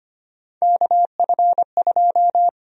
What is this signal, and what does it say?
Morse code, heard as a keyed tone, about 25 wpm, 700 Hz. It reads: KF2
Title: USA Call Sign Prefixes - 25wpm